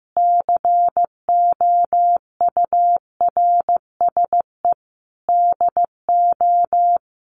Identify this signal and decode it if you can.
Morse code, heard as a keyed tone, about 14 wpm, 700 Hz. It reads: COURSE DO
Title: Sentences from Top 1200 Words 15wpm